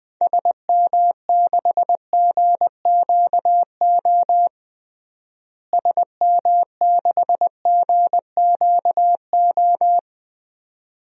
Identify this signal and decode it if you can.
Morse code, heard as a keyed tone, about 21 wpm, 700 Hz. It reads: SM6GQO SM6GQO
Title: DXCC-100 Call Signs - 6 Characters - Repeated 2x - 20wpm